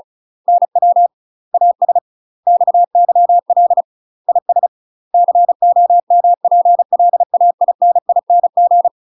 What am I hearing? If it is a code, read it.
Morse code, heard as a keyed tone, about 34 wpm, 700 Hz. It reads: NW AS XYL IS COMPLAINING
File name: UK MASTER QSOs TRAINING / 35WPM / UK MASTER QSOs 0041